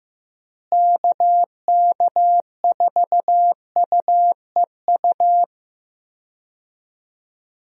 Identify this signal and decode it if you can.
Morse code, heard as a keyed tone, about 16 wpm, 700 Hz. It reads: KK4UEU
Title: USA Call Signs - 6 Characters - Repeated 2x - 15wpm